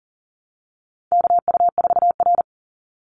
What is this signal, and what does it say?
Morse code, heard as a keyed tone, about 40 wpm, 700 Hz. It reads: KU4R
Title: NA SOTA Chaser Callsigns 40wpm